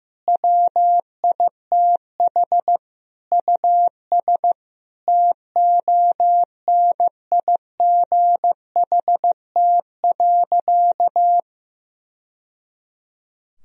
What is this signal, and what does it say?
Morse code, heard as a keyed tone, about 15 wpm, 700 Hz. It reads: WITH US TONIGHT.
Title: Sentences from Top 1000 Words - Review Part 3 - 15wpm